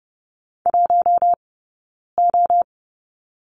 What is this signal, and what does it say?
Morse code, heard as a keyed tone, about 30 wpm, spaced as for 9 wpm, 690 Hz. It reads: 1O
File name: Lesson 17d - Letter u - Two Characters 30wpm